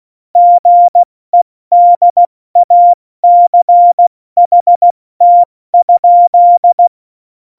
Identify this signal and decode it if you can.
Morse code, heard as a keyed tone, about 16 wpm, spaced as for 14 wpm, 700 Hz. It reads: GEDACHT?
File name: Morsekurs / Witze / Witz 19. September 17, Tempo 70